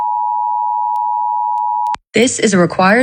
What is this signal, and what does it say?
Emergency Alert System (SAME) tones (the 853 and 960 Hz attention tone), with speech. It is heard from 0.0 to 2.0 s.